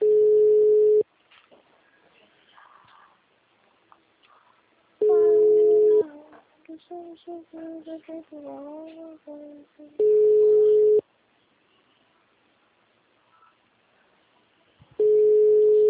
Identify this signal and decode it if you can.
Ringback tone (Europe, 425 Hz), with speech.